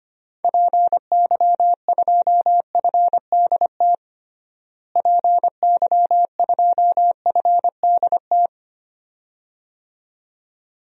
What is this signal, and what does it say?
Morse code, heard as a keyed tone, about 25 wpm, 700 Hz. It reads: PY2FDT PY2FDT
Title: DXCC-100 Call Signs - 6 Characters - Repeated 2x - 25wpm